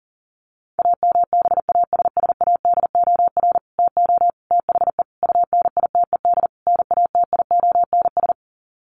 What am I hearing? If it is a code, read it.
Morse code, heard as a keyed tone, about 37 wpm, 700 Hz. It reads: AMBASSADOR TO THE UNITED NATIONS
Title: Sets of 5 Words Encore2 40wpm